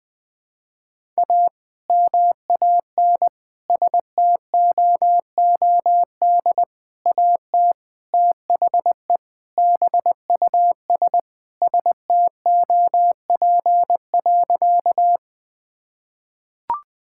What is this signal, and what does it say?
Morse code, heard as a keyed tone, about 21 wpm, 700 Hz. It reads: A MAN STOOD AT THE BUS STOP.
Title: Sentences with 2-Word Phrases - 20wpm